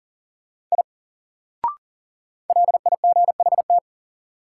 Morse code, heard as a keyed tone, about 40 wpm, 700 Hz. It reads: I LIGHT
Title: Sentences from Top 200 Words - New Word Drill - 40wpm